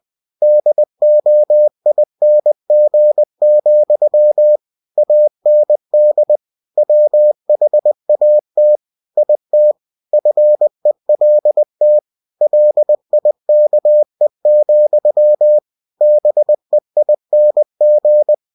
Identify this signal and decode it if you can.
Morse code, heard as a keyed tone, about 20 wpm, 600 Hz. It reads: DOING, AND WHAT IT FELT LIKE, BEING
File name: cw-books / winnie-the-pooh / winnie-20wpm-page-0013